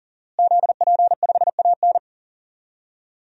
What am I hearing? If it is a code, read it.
Morse code, heard as a keyed tone, about 40 wpm, 700 Hz. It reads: ZP5AN
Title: DXCC-100 Call Signs - 5 Characters - Repeated 3x - 40wpm